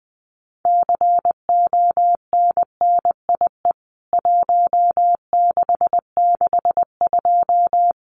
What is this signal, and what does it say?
Morse code, heard as a keyed tone, about 20 wpm, 700 Hz. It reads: CONNIE 1662
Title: CWT Exchange 20wpm